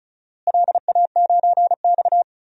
Morse code, heard as a keyed tone, about 34 wpm, 700 Hz. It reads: LA9X
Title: DXCC-100 Call Signs - 4 Characters - Repeated 2x - 35wpm